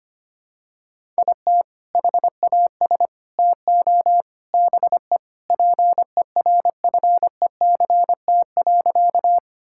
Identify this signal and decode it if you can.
Morse code, heard as a keyed tone, about 25 wpm, 700 Hz. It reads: IT HAS TO BE PERFECT.
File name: practice / 25wpm / Sentences with 3-Word Phrases - 25wpm